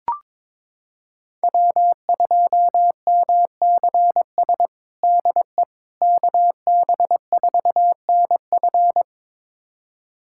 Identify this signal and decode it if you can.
Morse code, heard as a keyed tone, about 22 wpm, 700 Hz. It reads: W2MCS DE KB4NF
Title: USA Callsign DE Callsign - 5 Characters - 22wpm